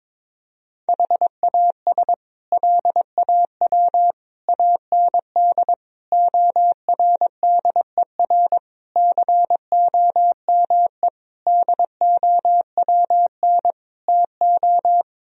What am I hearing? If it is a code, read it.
Morse code, heard as a keyed tone, about 21 wpm, 700 Hz. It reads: HAS LAW AND ORDER COME DOWN TO
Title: Sentences from Top 1000 Words - Review Part 3 - 22wpm